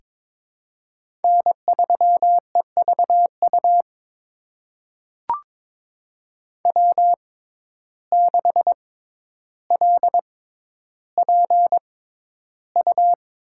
Morse code, heard as a keyed tone, about 22 wpm, spaced as for 11 wpm, 700 Hz. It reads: N3EVU W6LPU
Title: USA Call Signs - 5 Characters - 6x Character Spacing - 22wpm